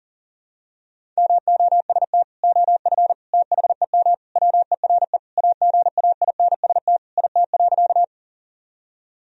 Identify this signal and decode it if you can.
Morse code, heard as a keyed tone, about 40 wpm, 700 Hz. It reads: MOST OF THEM WERE AGAINST IT.
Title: Sentences from Top 1000 Words - Review Part 1 - 40wpm